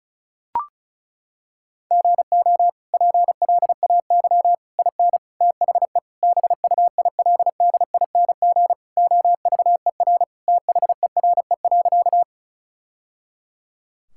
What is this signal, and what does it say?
Morse code, heard as a keyed tone, about 36 wpm, 700 Hz. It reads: GO PLAY IN THE BUILDING OVER THERE.
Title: Sentences from Top 1000 Words - Review Part 5 - 35wpm